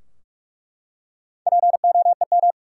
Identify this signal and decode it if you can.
Morse code, heard as a keyed tone, about 46 wpm, 700 Hz. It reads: POEM
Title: Four Letter Words - 45wpm